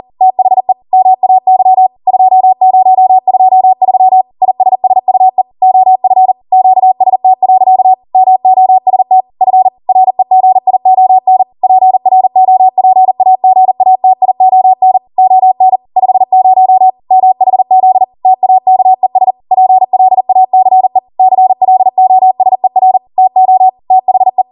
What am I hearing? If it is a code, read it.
Morse code, heard as a keyed tone, about 40 wpm, 750 Hz. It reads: THE MAY 2023 ISSUE OF QST. MOST F REGION PROPAGATION ON 50 MHZ TAKES PLACE CLOSER TO THE